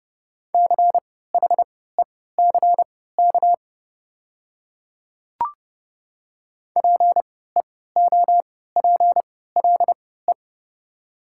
Morse code, heard as a keyed tone, about 30 wpm, spaced as for 17 wpm, 700 Hz. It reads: CHECK PEOPLE
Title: 3x Character Spacing - Top 500 Words - 30wpm